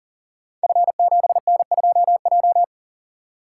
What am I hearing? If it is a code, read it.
Morse code, heard as a keyed tone, about 39 wpm, 700 Hz. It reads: F7N2J
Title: Code Groups - 5 Letter:Numbers - 40wpm